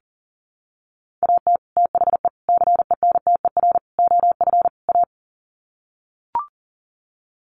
Morse code, heard as a keyed tone, about 37 wpm, 700 Hz. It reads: AT THE CENTER OF A
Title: Sets of 5 Words Encore2 40wpm